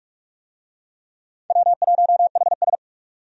Morse code, heard as a keyed tone, about 46 wpm, 700 Hz. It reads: W1HS
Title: USA Call Signs - 4 Characters - 45wpm